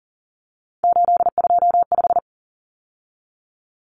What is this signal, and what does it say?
Morse code, heard as a keyed tone, about 40 wpm, 700 Hz. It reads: 825
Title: Common Cut Numbers - 3-Digit 40wpm